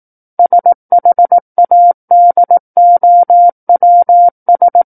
Morse code, heard as a keyed tone, about 18 wpm, 700 Hz. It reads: SHADOWS
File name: 18 WPM